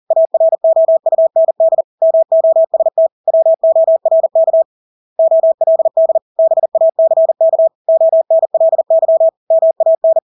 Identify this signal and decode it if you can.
Morse code, heard as a keyed tone, about 40 wpm, 650 Hz. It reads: AROUND MOST WORK OLD BACK ONLY MAN
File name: Engelske ord 40-40 wpm